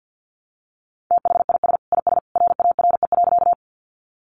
Morse code, heard as a keyed tone, about 50 wpm, 700 Hz. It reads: THIS IS RARE.
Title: Sentences from Top 1100 Words 50wpm